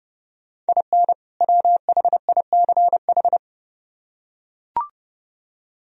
Morse code, heard as a keyed tone, about 30 wpm, 700 Hz. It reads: IN WHICH